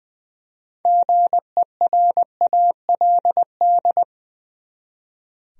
Morse code, heard as a keyed tone, about 20 wpm, 700 Hz. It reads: GERALD
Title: USA Common Names - 20wpm